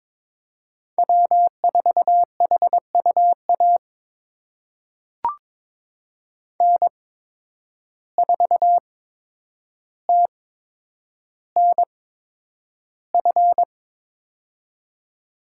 Morse code, heard as a keyed tone, about 22 wpm, spaced as for 9 wpm, 700 Hz. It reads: W4HUA N4TNF
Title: USA Call Signs - 5 Characters - 8x Character Spacing - 22wpm